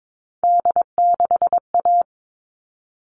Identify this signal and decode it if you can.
Morse code, heard as a keyed tone, about 22 wpm, 700 Hz. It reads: D6A
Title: All Cut Numbers - 3-Digit 22wpm